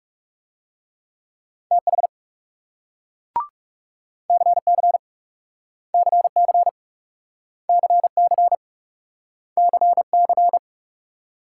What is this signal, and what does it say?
Morse code, heard as a keyed tone, about 41 wpm, 700 Hz. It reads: TH CC CC CC CC